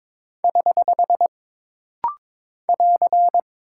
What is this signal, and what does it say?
Morse code, heard as a keyed tone, about 22 wpm, 700 Hz. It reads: <HH> <AR>
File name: practice / 22wpm / Single Punctuation-Prosign - Rapid-Fire - 22wpm